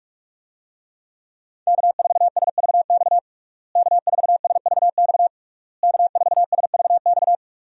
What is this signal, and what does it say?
Morse code, heard as a keyed tone, about 46 wpm, 700 Hz. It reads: K4SVX K4SVX K4SVX
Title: USA Call Signs - 5 Characters - Repeated 3x - 45wpm